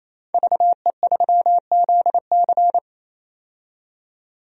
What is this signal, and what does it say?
Morse code, heard as a keyed tone, about 28 wpm, 700 Hz. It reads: VE3ZC
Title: DXCC-100 Call Signs - 5 Characters - 28wpm